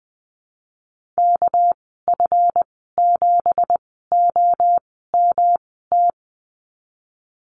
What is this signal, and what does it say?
Morse code, heard as a keyed tone, about 20 wpm, spaced as for 15 wpm, 700 Hz. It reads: KF7OMT
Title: 231 Lesson 37g - Letter x - Call Signs 20wpm